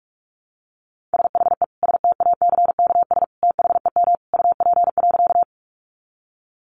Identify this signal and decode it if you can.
Morse code, heard as a keyed tone, about 44 wpm, 700 Hz. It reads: SHE STACKS THEM UP.
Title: Easy Phrases 45wpm